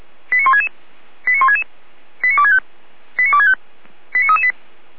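Five-tone selective call (ZVEI), heard as quick runs of five tones, tones of about 70 ms. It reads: 89166 89166 89265 89265 89338
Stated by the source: ZVEI five-tone selective call received on SDR.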